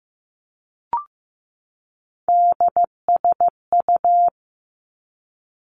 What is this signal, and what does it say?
Morse code, heard as a keyed tone, about 15 wpm, 700 Hz. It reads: DSU
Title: Three Challenging Letter-Number Combinations 15wpm